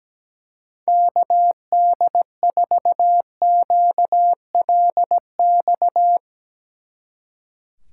Morse code, heard as a keyed tone, about 19 wpm, 700 Hz. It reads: KD4QLX
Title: Speed Racing - USA 2x3 Call Signs 17wpm